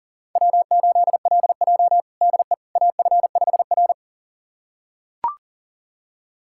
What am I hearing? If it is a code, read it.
Morse code, heard as a keyed tone, about 40 wpm, 700 Hz. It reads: W8LJ DE AF5R
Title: USA Callsign DE Callsign - 4 Characters - 40wpm